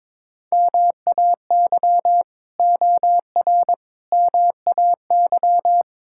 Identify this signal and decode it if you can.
Morse code, heard as a keyed tone, about 22 wpm, 700 Hz. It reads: MAY OR MAY
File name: practice / 22wpm / Build A Phrase - 5-Word Phrases - 22wpm